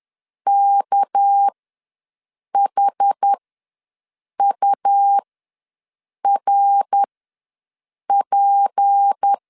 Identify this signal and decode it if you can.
Morse code, heard as a keyed tone, about 10 wpm, spaced as for 6 wpm, 800 Hz. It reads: KHURP
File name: LEZIONI / 8 YJ45 / 8 YJ ws 10wpm 4043